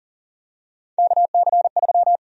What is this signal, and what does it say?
Morse code, heard as a keyed tone, about 40 wpm, 700 Hz. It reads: KC3
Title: USA Call Sign Prefixes - 40wpm